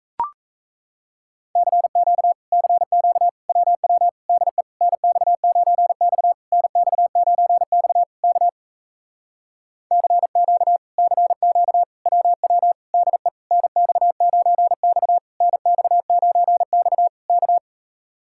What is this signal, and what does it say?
Morse code, heard as a keyed tone, about 38 wpm, 700 Hz. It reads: CQ CQ WW DE NX9X NX9X K CQ CQ WW DE NX9X NX9X K